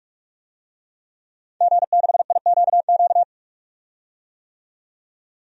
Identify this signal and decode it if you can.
Morse code, heard as a keyed tone, about 46 wpm, 700 Hz. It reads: G6IQQ
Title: DXCC-100 Call Signs - 5 Characters - 45wpm